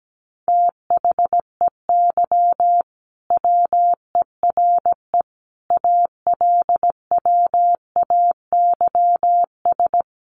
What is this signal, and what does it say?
Morse code, heard as a keyed tone, about 17 wpm, 700 Hz. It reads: THEY WERE ALWAYS
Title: Sentences from Top 1400 Words 17wpm